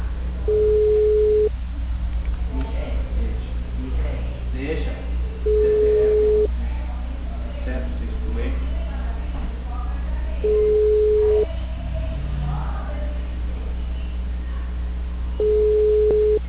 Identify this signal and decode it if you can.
Ringback tone (Europe, 425 Hz), with speech.